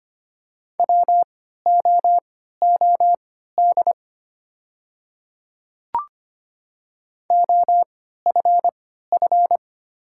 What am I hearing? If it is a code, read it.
Morse code, heard as a keyed tone, about 25 wpm, spaced as for 14 wpm, 700 Hz. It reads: WOOD OFF
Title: 3x Character Spacing - Top 300 Words - 25wpm